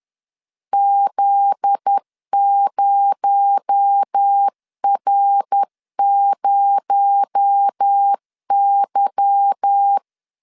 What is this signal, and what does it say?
Morse code, heard as a keyed tone, about 10 wpm, 800 Hz. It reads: Z0R0Y
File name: CW / DET / 1 9 1500C S A 10wpm 4511